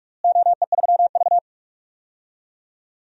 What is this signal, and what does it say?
Morse code, heard as a keyed tone, about 44 wpm, 700 Hz. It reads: OE3V
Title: DXCC-100 Call Signs - 4 Characters - Repeated 3x - 45wpm